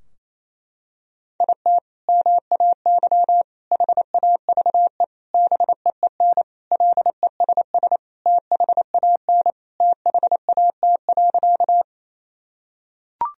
Morse code, heard as a keyed tone, about 28 wpm, 700 Hz. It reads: IT MAY HAVE BEEN LESS THAN THAT.
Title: Sentences from Top 1000 Words - Review Part 2 - 28wpm